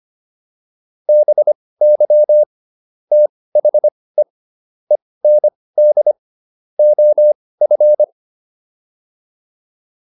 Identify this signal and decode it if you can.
Morse code, heard as a keyed tone, about 25 wpm, spaced as for 18 wpm, 600 Hz. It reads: BY THE END OF